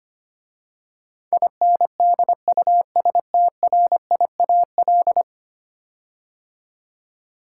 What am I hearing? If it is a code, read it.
Morse code, heard as a keyed tone, about 25 wpm, 700 Hz. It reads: INDUSTRIAL
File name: 10 Letter Words - First 3 Letters Given - 25wpm